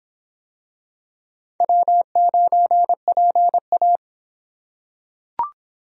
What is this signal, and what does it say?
Morse code, heard as a keyed tone, about 23 wpm, 700 Hz. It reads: W9PA